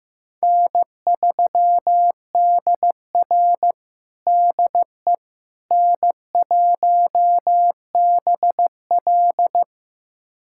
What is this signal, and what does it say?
Morse code, heard as a keyed tone, about 16 wpm, 700 Hz. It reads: N3DR DE N1BL
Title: USA Callsign DE Callsign - 4 Characters - 15wpm